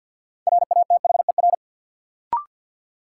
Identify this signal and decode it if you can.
Morse code, heard as a keyed tone, about 52 wpm, 700 Hz. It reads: RATHER
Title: Sentences from Top 700 Words - New Word Drill - 50wpm